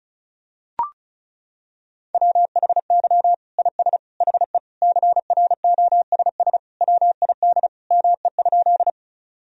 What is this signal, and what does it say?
Morse code, heard as a keyed tone, about 36 wpm, 700 Hz. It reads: WHY IS HE CROSS WID ME?
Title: Sentences from Top 300 Words and Abbreviations 35wpm